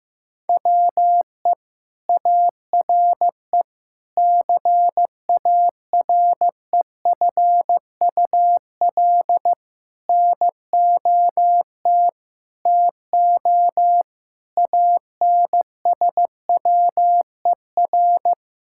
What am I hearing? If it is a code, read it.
Morse code, heard as a keyed tone, about 15 wpm, 700 Hz. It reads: WE ARE CAREFUL NOT TO ANSWER
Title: Sentences from Top 1600 Words 15wpm